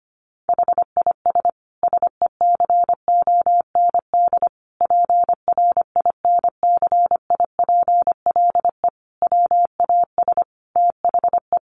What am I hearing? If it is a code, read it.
Morse code, heard as a keyed tone, about 25 wpm, 700 Hz. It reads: HIS SECOND PRINCIPLE WAS THE
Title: Sentences from Top 1400 Words 25wpm